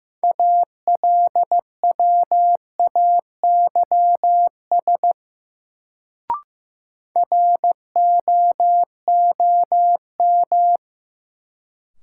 Morse code, heard as a keyed tone, about 15 wpm, 700 Hz. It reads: ALWAYS ROOM
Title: Sentences from Top 300 Words - New Word Drill - 15wpm